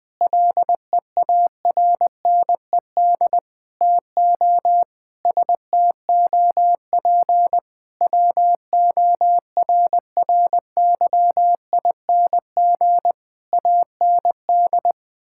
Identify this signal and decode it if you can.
Morse code, heard as a keyed tone, about 20 wpm, 700 Hz. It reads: LEARNED TO STOP WORRYING AND
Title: Sets of 8 Words - 20wpm